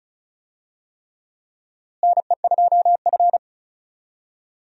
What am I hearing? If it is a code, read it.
Morse code, heard as a keyed tone, about 36 wpm, 700 Hz. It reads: NE2F
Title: USA Call Signs - 4 Characters - Repeated 3x - 35wpm